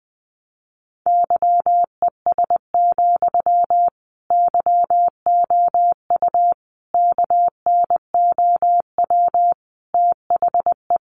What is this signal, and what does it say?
Morse code, heard as a keyed tone, about 20 wpm, 700 Hz. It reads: YES, YOU KNOW THE
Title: Sentences from Top 700 Words 20wpm